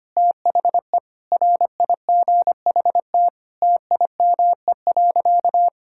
Morse code, heard as a keyed tone, about 25 wpm, 700 Hz. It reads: THE RIGHT TIME.
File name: practice / 25wpm / Sentences with 9-Word Phrases - 25wpm